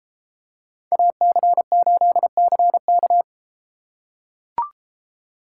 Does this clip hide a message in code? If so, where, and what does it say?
Yes: Morse code from 0.9 to 3.2 s, heard as a keyed tone, about 32 wpm, 700 Hz. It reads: AC8CK